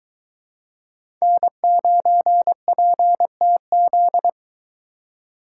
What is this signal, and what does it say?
Morse code, heard as a keyed tone, about 23 wpm, 700 Hz. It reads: N9PTZ